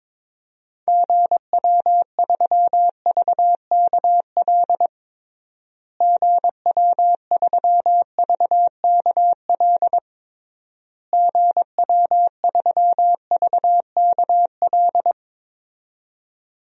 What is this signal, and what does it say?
Morse code, heard as a keyed tone, about 22 wpm, 700 Hz. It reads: GW3VKL GW3VKL GW3VKL
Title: DXCC-100 Call Signs - 6 Characters - Repeated 3x - 22wpm